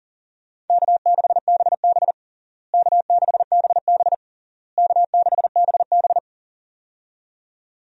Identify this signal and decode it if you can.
Morse code, heard as a keyed tone, about 37 wpm, 700 Hz. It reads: K6BB K6BB K6BB
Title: USA Call Signs - 4 Characters - Repeated 3x - 40wpm